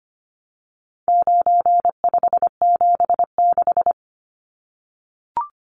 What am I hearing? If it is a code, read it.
Morse code, heard as a keyed tone, about 25 wpm, 700 Hz. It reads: 9576
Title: Numbers - 4-Digit 25wpm